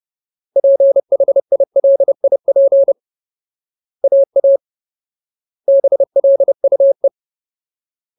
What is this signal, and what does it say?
Morse code, heard as a keyed tone, about 30 wpm, 550 Hz. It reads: PHILIP AA BLUE
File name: Ultimate Word List / 30WPM Ultimate